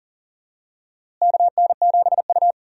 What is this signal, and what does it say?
Morse code, heard as a keyed tone, about 40 wpm, 700 Hz. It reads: KN7U
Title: USA Call Signs - 4 Characters - Repeated 3x - 40wpm